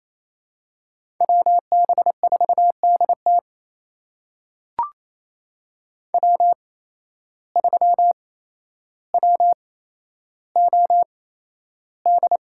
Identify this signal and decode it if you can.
Morse code, heard as a keyed tone, about 27 wpm, spaced as for 12 wpm, 700 Hz. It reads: WB4DT W3WOD
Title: USA Call Signs - 5 Characters - 8x Character Spacing - 28wpm